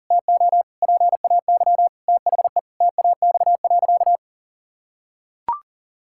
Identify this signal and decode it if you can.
Morse code, heard as a keyed tone, about 40 wpm, 700 Hz. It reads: TO PAY THE TAX.
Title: Sentences from Top 1000 Words - Review Part 6 - 40wpm